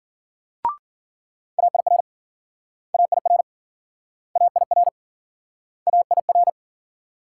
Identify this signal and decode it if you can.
Morse code, heard as a keyed tone, about 55 wpm, 700 Hz. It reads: AIR AIR AIR AIR